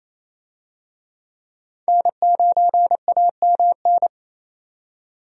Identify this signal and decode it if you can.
Morse code, heard as a keyed tone, about 28 wpm, 700 Hz. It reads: N9AMN
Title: USA Call Signs - 5 Characters - 28wpm